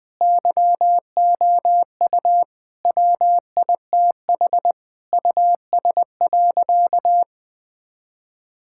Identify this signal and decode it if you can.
Morse code, heard as a keyed tone, about 19 wpm, 700 Hz. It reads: YOU WITH US.
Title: Sentences with 5-Word Phrases - Spoken Beginning - 20wpm